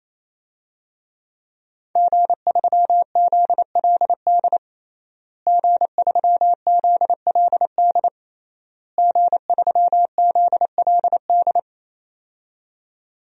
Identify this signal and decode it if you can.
Morse code, heard as a keyed tone, about 28 wpm, 700 Hz. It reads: G3ZLD G3ZLD G3ZLD
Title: DXCC-100 Call Signs - 5 Characters - Repeated 3x - 28wpm